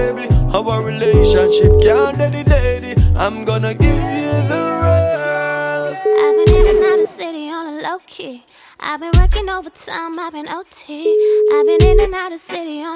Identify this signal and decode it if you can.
Ringback tone (Europe, 425 Hz), with speech.